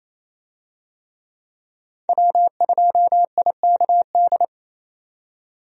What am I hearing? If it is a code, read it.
Morse code, heard as a keyed tone, about 28 wpm, 700 Hz. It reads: W2IKD
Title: USA Call Signs - 5 Characters - 28wpm